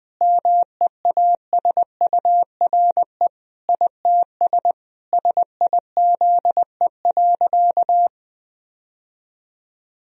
Morse code, heard as a keyed tone, about 21 wpm, 700 Hz. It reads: MEASURE ITS SIZE.
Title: Sentences from Top 1000 Words - Review Part 2 - 20wpm